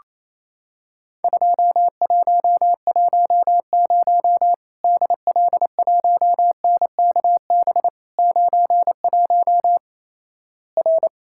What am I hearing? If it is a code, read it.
Morse code, heard as a keyed tone, about 28 wpm, 700 Hz. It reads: 2110 DL1NKB 91 R
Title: Worked All Europe DX Contest - QTC Exchange - 28wpm